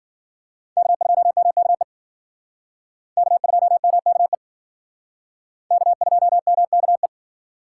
Morse code, heard as a keyed tone, about 56 wpm, 700 Hz. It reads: K2MKE K2MKE K2MKE